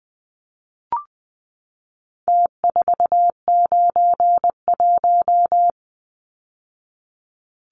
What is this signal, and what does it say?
Morse code, heard as a keyed tone, about 20 wpm, 700 Hz. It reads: T491
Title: Numbers - Contest Style - 4-Digit 20wpm